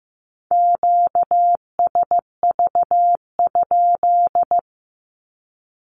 Morse code, heard as a keyed tone, about 15 wpm, 700 Hz. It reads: QSV?
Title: Q-Codes with Spoken Meaning - 15wpm